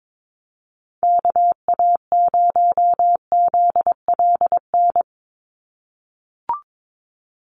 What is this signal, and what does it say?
Morse code, heard as a keyed tone, about 22 wpm, 700 Hz. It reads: KA0ZLN
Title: USA Call Signs 22wpm